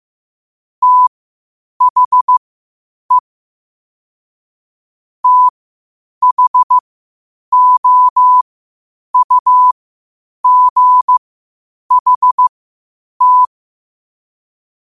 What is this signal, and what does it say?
Morse code, heard as a keyed tone, about 15 wpm, spaced as for 8 wpm, 1000 Hz. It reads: THE THOUGHT
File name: qotdfiles / 8wpm